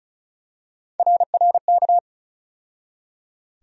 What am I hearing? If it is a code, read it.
Morse code, heard as a keyed tone, about 34 wpm, 700 Hz. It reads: RRK
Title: USA Call Sign Postfixes - 3 Letters 35wpm